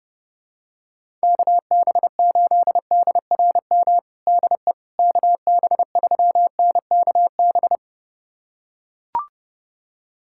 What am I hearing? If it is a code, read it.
Morse code, heard as a keyed tone, about 30 wpm, 700 Hz. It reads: KB8DRM DE KB3NKB
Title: USA Callsign DE Callsign - 6 Characters - 30wpm